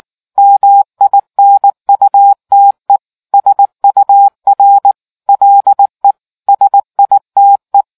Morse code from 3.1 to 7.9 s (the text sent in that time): SUR LE SITE